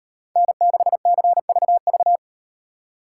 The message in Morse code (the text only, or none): N6CVV